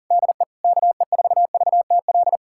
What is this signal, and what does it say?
Morse code, heard as a keyed tone, about 40 wpm, 700 Hz. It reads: DE KE4VTL
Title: USA Callsign DE Callsign - 6 Characters - 40wpm